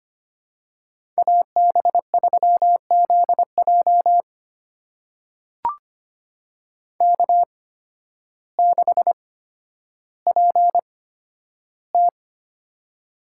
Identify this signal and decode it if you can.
Morse code, heard as a keyed tone, about 25 wpm, spaced as for 12 wpm, 700 Hz. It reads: AB3ZJ K6PT
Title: USA Call Signs - 8x Character Spacing - 25wpm